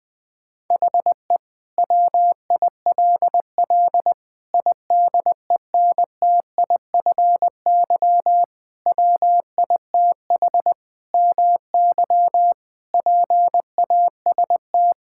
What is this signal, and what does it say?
Morse code, heard as a keyed tone, about 21 wpm, 700 Hz. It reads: HE WILL IDENTIFY WITH MY PAST
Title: Sentences from Top 1000 Words - Review Part 6 - 20wpm